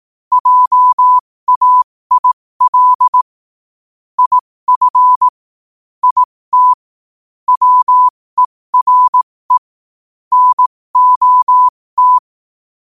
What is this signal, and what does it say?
Morse code, heard as a keyed tone, about 18 wpm, spaced as for 15 wpm, 1000 Hz. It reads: JAIL IF IT WERE NOT
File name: qotdfiles / 15wpm